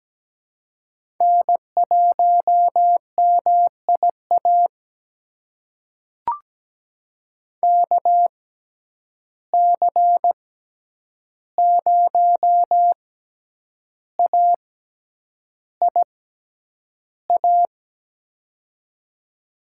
Morse code, heard as a keyed tone, about 17 wpm, spaced as for 8 wpm, 700 Hz. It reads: N1MIA KC0AIA
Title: USA Call Signs - 6x Character Spacing - 17wpm